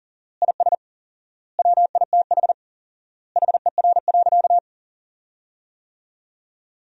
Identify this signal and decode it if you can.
Morse code, heard as a keyed tone, about 37 wpm, 700 Hz. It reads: IS WITH HER.
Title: 4x Word-Spacing - Sentences from Top 100 Words - 40wpm